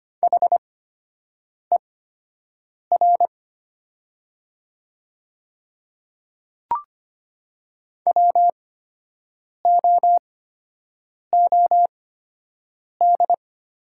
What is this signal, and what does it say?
Morse code, heard as a keyed tone, about 25 wpm, spaced as for 7 wpm, 700 Hz. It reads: HER WOOD
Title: 8x Character Spacing - Top 300 Words - 25wpm